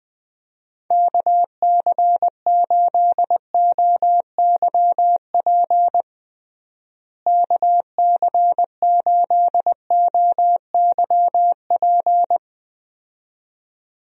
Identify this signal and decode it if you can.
Morse code, heard as a keyed tone, about 20 wpm, 700 Hz. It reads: KC8OYP KC8OYP
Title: USA Call Signs - 6 Characters - Repeated 2x - 20wpm